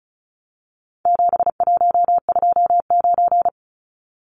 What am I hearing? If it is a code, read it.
Morse code, heard as a keyed tone, about 34 wpm, 700 Hz. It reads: 7129
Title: Numbers - 4-Digit 35wpm